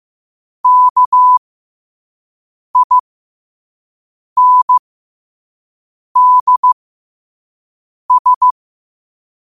Morse code, heard as a keyed tone, about 15 wpm, spaced as for 5 wpm, 1000 Hz. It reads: KINDS
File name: qotdfiles / 5wpm